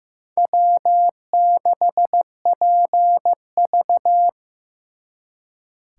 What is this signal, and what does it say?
Morse code, heard as a keyed tone, about 18 wpm, 700 Hz. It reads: W6PV
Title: Speed Racing - USA 1x2 Call Signs 15wpm